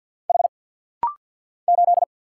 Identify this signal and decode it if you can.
Morse code, heard as a keyed tone, about 50 wpm, 700 Hz. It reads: H 8
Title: Single Letter-Number Alt - Rapid-Fire 50wpm